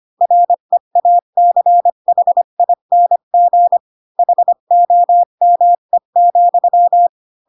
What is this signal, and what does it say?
Morse code, heard as a keyed tone, about 25 wpm, 700 Hz. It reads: REACHING HOME,
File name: Pinocchio / Pinocchio ch4 20 25 ik1hge